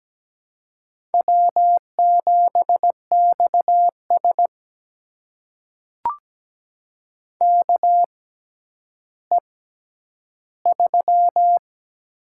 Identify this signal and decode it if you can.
Morse code, heard as a keyed tone, about 17 wpm, spaced as for 10 wpm, 700 Hz. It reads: W7XS KE3
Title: USA Call Signs - 6x Character Spacing - 17wpm